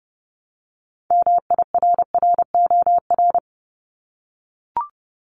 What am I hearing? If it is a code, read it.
Morse code, heard as a keyed tone, about 30 wpm, 700 Hz. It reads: MIRROR
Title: Multi-letter Phonemes Repeated 3x 30wpm